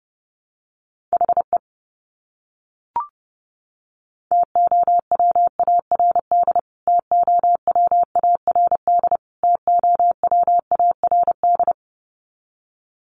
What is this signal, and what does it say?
Morse code, heard as a keyed tone, about 29 wpm, 700 Hz. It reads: HE TOWARD TOWARD TOWARD
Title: Top 500 Words Repeated 30wpm